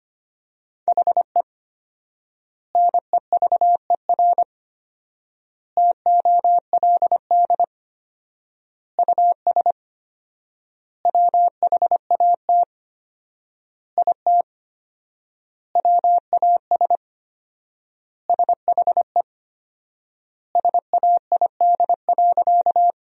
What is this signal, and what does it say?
Morse code, heard as a keyed tone, about 25 wpm, 700 Hz. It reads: HE NEVER TOLD US WHAT IT WAS SHE SAID.
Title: 4x Word-Spacing - Sentences from Top 400 Words - 25wpm